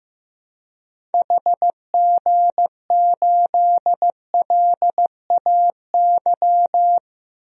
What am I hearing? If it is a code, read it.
Morse code, heard as a keyed tone, about 15 wpm, 700 Hz. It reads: HG8LAY
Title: DXCC-100 Call Signs - 6 Characters - Repeated 2x - 15wpm